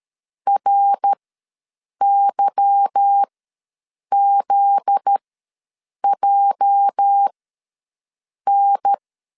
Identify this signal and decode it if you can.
Morse code, heard as a keyed tone, about 13 wpm, spaced as for 7 wpm, 800 Hz. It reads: RYZJN
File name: LEZIONI / 8 YJ45 / 8 YJ4 ws 12wpm 3249